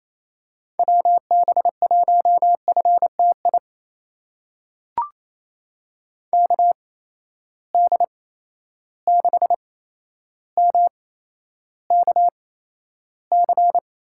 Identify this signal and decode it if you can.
Morse code, heard as a keyed tone, about 27 wpm, spaced as for 12 wpm, 700 Hz. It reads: WB1FTI KD6MKC
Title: USA Call Signs - 6 Characters - 8x Character Spacing - 28wpm